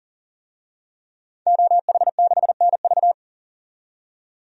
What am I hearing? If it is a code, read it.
Morse code, heard as a keyed tone, about 40 wpm, 700 Hz. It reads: OH6NV
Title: DXCC-100 Call Signs - 5 Characters - 40wpm